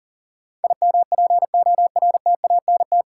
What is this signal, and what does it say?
Morse code, heard as a keyed tone, about 40 wpm, 700 Hz. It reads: IMPORTANT
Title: Sentences from Top 600 Words - New Word Drill - 40wpm